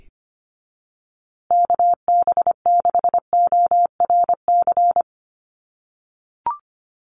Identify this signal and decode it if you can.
Morse code, heard as a keyed tone, about 25 wpm, 700 Hz. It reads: KB6ORC
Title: USA Call Signs Repeated 2x 25wpm